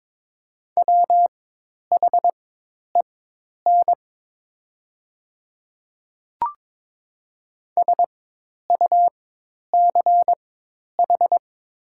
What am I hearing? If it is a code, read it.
Morse code, heard as a keyed tone, about 22 wpm, spaced as for 10 wpm, 700 Hz. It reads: WHEN SUCH
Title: 4x Character Spacing - Top 300 Words - 22wpm